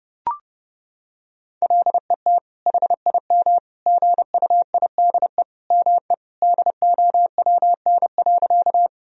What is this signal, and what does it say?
Morse code, heard as a keyed tone, about 32 wpm, 700 Hz. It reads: LET HIM GUIDE ME DOWN.
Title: Sentences from Top 1000 Words - Review Part 3 - 30wpm